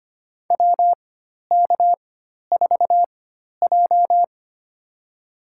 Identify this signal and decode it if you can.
Morse code, heard as a keyed tone, about 25 wpm, spaced as for 12 wpm, 700 Hz. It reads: WK4J
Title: USA Call Signs - 4x Character Spacing - 25wpm